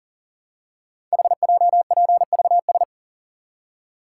Morse code, heard as a keyed tone, about 38 wpm, 700 Hz. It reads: HJPVS
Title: Code Groups - 5 Letters - 40wpm